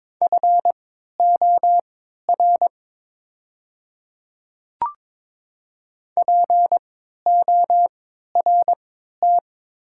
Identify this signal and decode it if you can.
Morse code, heard as a keyed tone, about 22 wpm, spaced as for 13 wpm, 700 Hz. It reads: FOR PORT
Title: 3x Character Spacing - Top 300 Words - 22wpm